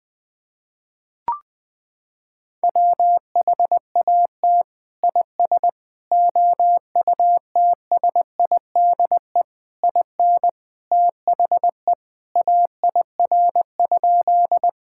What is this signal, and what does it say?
Morse code, heard as a keyed tone, about 20 wpm, 700 Hz. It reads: WHAT IS OUTSIDE IN THE AIR?
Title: Sentences from Top 1000 Words - Review Part 6 - 20wpm